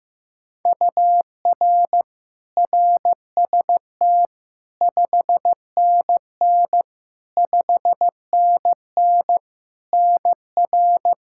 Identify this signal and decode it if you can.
Morse code, heard as a keyed tone, about 15 wpm, 700 Hz. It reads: UR RST 5NN 5NN NR
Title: Practice QSO - Exchange 1 - Beginning - 15wpm